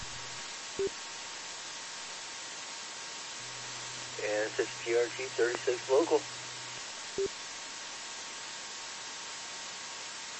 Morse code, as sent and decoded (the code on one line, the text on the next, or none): none